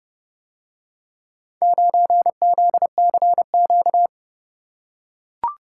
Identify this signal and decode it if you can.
Morse code, heard as a keyed tone, about 29 wpm, 700 Hz. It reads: 9ZCQ